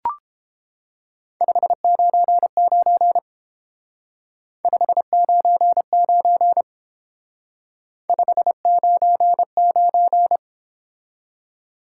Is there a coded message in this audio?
Yes: Morse code from 1.4 to 10.4 s, heard as a keyed tone, about 29 wpm, 700 Hz. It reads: 599 599 599